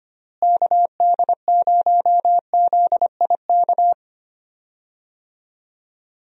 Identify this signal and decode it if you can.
Morse code, heard as a keyed tone, about 25 wpm, 700 Hz. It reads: KD0ZIK
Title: USA Call Signs - 6 Characters - Repeated 3x - 25wpm